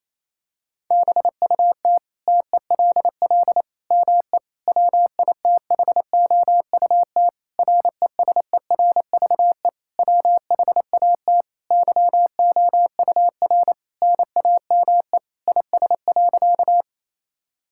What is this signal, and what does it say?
Morse code, heard as a keyed tone, about 28 wpm, 700 Hz. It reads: BUT TELL ME WITHOUT RESERVE WHAT YOUR NAME IS.
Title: Sentences from Top 1000 Words - Review Part 6 - 28wpm